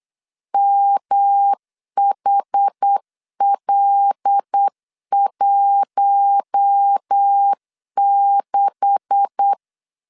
Morse code, heard as a keyed tone, about 8 wpm, 800 Hz. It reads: MHL16
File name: LEZIONI / 9 286 8wpm 3425